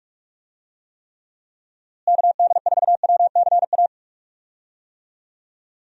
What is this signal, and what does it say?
Morse code, heard as a keyed tone, about 47 wpm, 700 Hz. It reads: KD4WCA
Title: USA Call Signs - 6 Characters - Repeated 2x - 45wpm